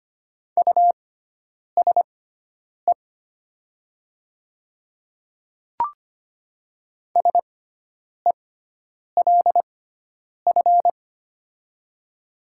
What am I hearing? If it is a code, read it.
Morse code, heard as a keyed tone, about 25 wpm, spaced as for 9 wpm, 700 Hz. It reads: USE SELF
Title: 6x Character Spacing - Top 300 Words - 25wpm